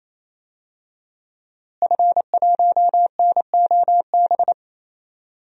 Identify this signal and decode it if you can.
Morse code, heard as a keyed tone, about 27 wpm, 700 Hz. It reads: F1NOB
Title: DXCC-100 Call Signs - 5 Characters - Repeated 3x - 28wpm